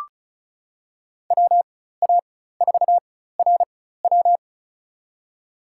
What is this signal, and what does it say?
Morse code, heard as a keyed tone, about 34 wpm, spaced as for 16 wpm, 700 Hz. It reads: WA4RW
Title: USA Call Signs - 5 Characters - 4x Character Spacing - 35wpm